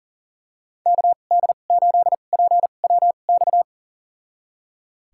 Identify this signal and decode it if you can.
Morse code, heard as a keyed tone, about 38 wpm, spaced as for 28 wpm, 700 Hz. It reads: KD8PWX
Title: USA Call Signs - 6 Characters - 2x Character Spacing - 40wpm